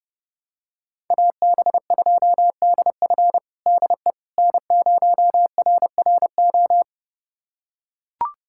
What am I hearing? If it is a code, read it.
Morse code, heard as a keyed tone, about 30 wpm, 700 Hz. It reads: AB2DF DE N0RRO
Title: USA Callsign DE Callsign 30wpm